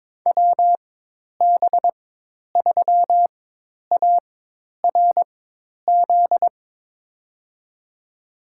Morse code, heard as a keyed tone, about 21 wpm, spaced as for 10 wpm, 700 Hz. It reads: WB3ARZ
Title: USA Call Signs - 6 Characters - 4x Character Spacing - 22wpm